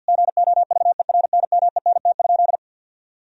Morse code, heard as a keyed tone, about 50 wpm, 700 Hz. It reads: GOVERNMENT?
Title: Sentences from Top 1000 Words - Review Part 4 - 50wpm